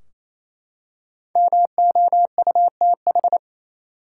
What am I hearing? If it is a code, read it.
Morse code, heard as a keyed tone, about 28 wpm, 700 Hz. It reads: MOUTH
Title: Five Letter Words - 28wpm